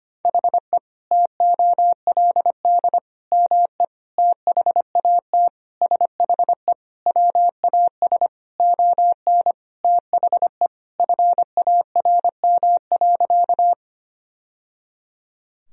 Morse code, heard as a keyed tone, about 25 wpm, 700 Hz. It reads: HE TOLD ME THAT SHE WAS ON THE FARM.
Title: Sentences from Top 1000 Words - Review Part 2 - 25wpm